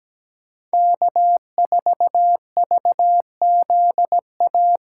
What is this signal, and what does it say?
Morse code, heard as a keyed tone, about 17 wpm, 700 Hz. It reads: K4VZA
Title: USA Callsign DE Callsign - 5 Characters - 17wpm